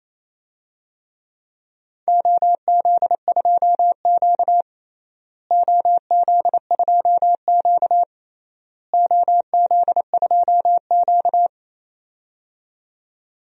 Morse code, heard as a keyed tone, about 28 wpm, 700 Hz. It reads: OZ2Q OZ2Q OZ2Q
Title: DXCC-100 Call Signs - 4 Characters - Repeated 3x - 28wpm